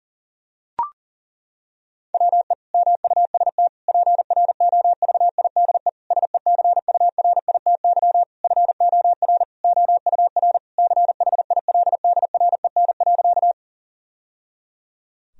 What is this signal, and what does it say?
Morse code, heard as a keyed tone, about 37 wpm, 700 Hz. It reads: WE MUST PROVIDE SECURITY FOR OUR CHILDREN.
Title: Sentences from Top 1000 Words - Review Part 4 - 40wpm